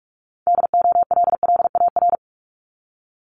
Morse code, heard as a keyed tone, about 46 wpm, 700 Hz. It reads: DOLLAR
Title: Multi-letter Phonemes 45wpm